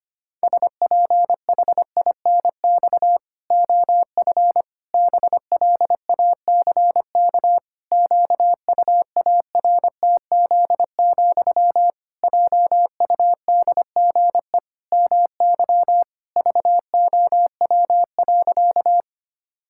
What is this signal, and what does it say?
Morse code, heard as a keyed tone, about 25 wpm, 700 Hz. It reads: SPHINX OF BLACK QUARTZ, JUDGE MY VOW.
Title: Pangram Challenge v2 - 25wpm